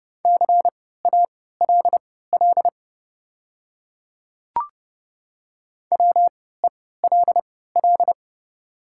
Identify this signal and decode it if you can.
Morse code, heard as a keyed tone, about 30 wpm, spaced as for 17 wpm, 700 Hz. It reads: CALL WELL
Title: 3x Character Spacing - Top 200 Words - 30wpm